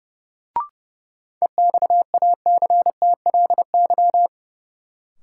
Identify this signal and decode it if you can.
Morse code, heard as a keyed tone, about 32 wpm, 700 Hz. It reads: EXACTLY
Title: Sentences from Top 900 Words - New Word Drill - 30wpm